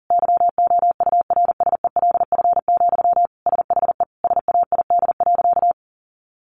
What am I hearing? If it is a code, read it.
Morse code, heard as a keyed tone, about 40 wpm, 700 Hz. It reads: YOURSELF, SHE SAID.
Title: Sentences from Top 600 Words 40wpm